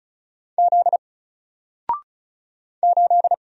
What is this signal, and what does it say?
Morse code, heard as a keyed tone, about 36 wpm, 700 Hz. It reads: Z 8
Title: Single Letter-Number Alt 35wpm